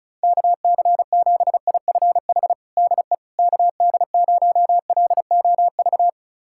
Morse code, heard as a keyed tone, about 34 wpm, 700 Hz. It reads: KC7IFH DE KD0LOV
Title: USA Callsign DE Callsign 35wpm